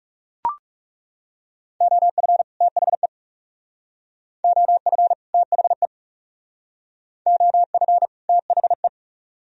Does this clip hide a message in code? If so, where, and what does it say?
Yes: Morse code from 1.8 to 8.9 s, heard as a keyed tone, about 40 wpm, 700 Hz. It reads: OF THE OF THE OF THE